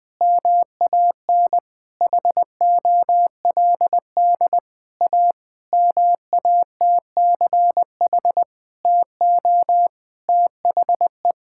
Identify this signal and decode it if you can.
Morse code, heard as a keyed tone, about 21 wpm, 700 Hz. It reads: MAN HOLD A MATCH TO THE
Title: Sentences from Top 1600 Words 20wpm